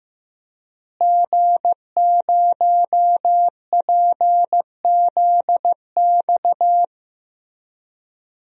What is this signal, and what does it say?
Morse code, heard as a keyed tone, about 16 wpm, 700 Hz. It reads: G0PZX
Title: DXCC-100 Call Signs - 5 Characters - Repeated 3x - 15wpm